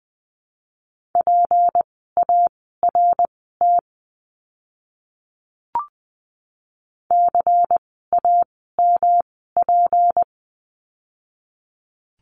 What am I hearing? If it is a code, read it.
Morse code, heard as a keyed tone, about 20 wpm, spaced as for 14 wpm, 700 Hz. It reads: PART CAMP
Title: 153 Lesson 25f - Letter p - Words 20wpm